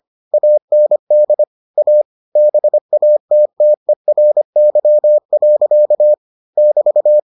Morse code, heard as a keyed tone, about 25 wpm, 600 Hz. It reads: AND A BATTERY. <BT>